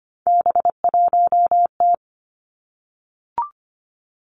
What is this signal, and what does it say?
Morse code, heard as a keyed tone, about 25 wpm, 700 Hz. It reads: B1T
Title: Three Challenging Letter-Number Combinations 25wpm